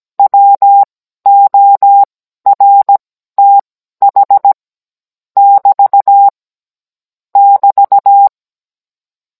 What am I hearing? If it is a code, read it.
Morse code, heard as a keyed tone, about 17 wpm, spaced as for 12 wpm, 800 Hz. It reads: WORTH <BT> <BT>
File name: Cours-NR19-16mm